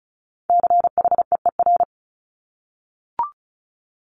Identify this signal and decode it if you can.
Morse code, heard as a keyed tone, about 36 wpm, 700 Hz. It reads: CHEER
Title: Multi-letter Phonemes Repeated 3x 35wpm